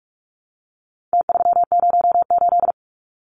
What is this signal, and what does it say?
Morse code, heard as a keyed tone, about 44 wpm, 700 Hz. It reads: T308
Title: Numbers - Contest Style - 4-Digit 45wpm